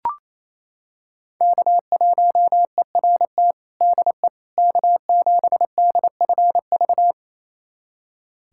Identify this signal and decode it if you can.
Morse code, heard as a keyed tone, about 28 wpm, 700 Hz. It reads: K1ERT DE K7DFV
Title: USA Callsign DE Callsign - 5 Characters - 28wpm